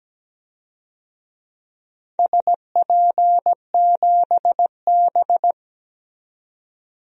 Morse code, heard as a keyed tone, about 17 wpm, 700 Hz. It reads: SP7B
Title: DXCC-100 Call Signs - 4 Characters - 17wpm